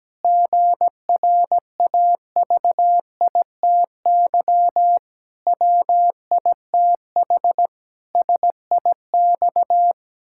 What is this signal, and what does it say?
Morse code, heard as a keyed tone, about 17 wpm, 700 Hz. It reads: GRAVITY WITH SIX
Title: Pangram Challenge v2 - 17wpm